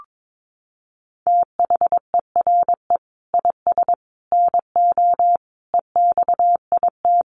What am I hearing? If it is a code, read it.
Morse code, heard as a keyed tone, about 22 wpm, 700 Hz. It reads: THERE IS NO EXIT
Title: Sentences from Top 1100 Words 22wpm